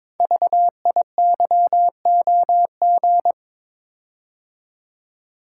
Morse code, heard as a keyed tone, about 22 wpm, 700 Hz. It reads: VIYOG
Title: Code Groups - 5 Letters - 22wpm